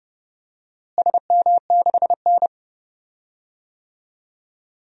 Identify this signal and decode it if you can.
Morse code, heard as a keyed tone, about 32 wpm, 700 Hz. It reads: SM6N
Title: DXCC-100 Call Signs - 4 Characters - Repeated 2x - 30wpm